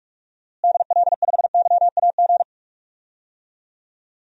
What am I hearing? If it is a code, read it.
Morse code, heard as a keyed tone, about 46 wpm, 700 Hz. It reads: DL5YAG
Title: DXCC-100 Call Signs - 6 Characters - Repeated 3x - 45wpm